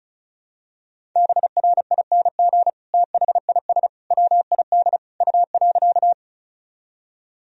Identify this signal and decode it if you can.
Morse code, heard as a keyed tone, about 34 wpm, 700 Hz. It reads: BRING THIS WID U.
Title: Sentences from Top 500 Words and Abbreviations 35wpm